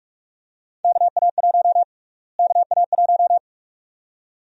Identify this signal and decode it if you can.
Morse code, heard as a keyed tone, about 46 wpm, 700 Hz. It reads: KA1 KA1
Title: USA Call Sign Prefixes - Repeated 2x - 45wpm